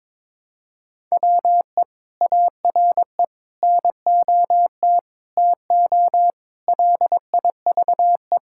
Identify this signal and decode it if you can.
Morse code, heard as a keyed tone, about 21 wpm, 700 Hz. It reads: WE ARE NOT TO LIVE
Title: Sentences from Top 1600 Words 22wpm